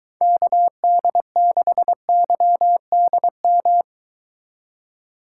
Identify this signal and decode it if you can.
Morse code, heard as a keyed tone, about 22 wpm, 700 Hz. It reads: KD6YDM